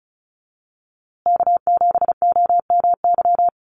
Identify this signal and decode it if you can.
Morse code, heard as a keyed tone, about 34 wpm, 700 Hz. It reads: K7OMY
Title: USA Call Signs Repeated 3x 35wpm